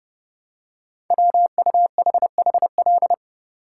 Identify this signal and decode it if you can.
Morse code, heard as a keyed tone, about 28 wpm, 700 Hz. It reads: WUHHL
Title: Code Groups - 5 Letters - 30wpm